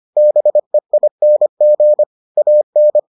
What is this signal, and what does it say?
Morse code, heard as a keyed tone, about 25 wpm, 600 Hz. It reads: BEING AN